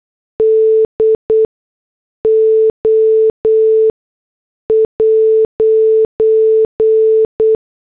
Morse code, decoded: DO'